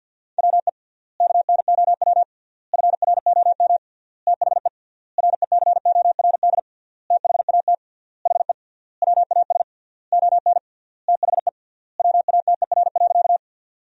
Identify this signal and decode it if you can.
Morse code, heard as a keyed tone, about 50 wpm, 700 Hz. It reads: WE KNOW FROM THE RECORD THAT HE WAS ON THE WATER.
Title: 3x Word-Spacing - Sentences from Top 500 Words - 50wpm